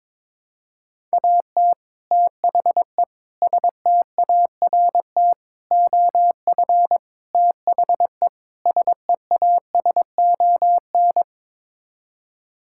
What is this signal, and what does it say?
Morse code, heard as a keyed tone, about 21 wpm, 700 Hz. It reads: AT THE START OF THE SEASON
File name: practice / 22wpm / Sets of 6 Words - 22wpm